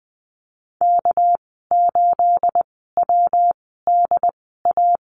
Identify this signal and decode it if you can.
Morse code, heard as a keyed tone, about 20 wpm, spaced as for 15 wpm, 700 Hz. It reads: K8WDA
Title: 231 Lesson 37g - Letter x - Call Signs 20wpm